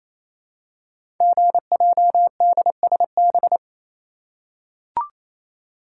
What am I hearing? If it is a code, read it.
Morse code, heard as a keyed tone, about 27 wpm, 700 Hz. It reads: GJDSB
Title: Code Groups - 5 Letters - 28wpm